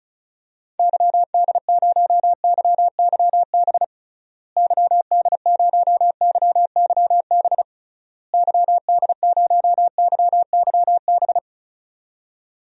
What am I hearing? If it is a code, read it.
Morse code, heard as a keyed tone, about 36 wpm, 700 Hz. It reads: YD0YYB YD0YYB YD0YYB
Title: DXCC-100 Call Signs - 6 Characters - Repeated 3x - 35wpm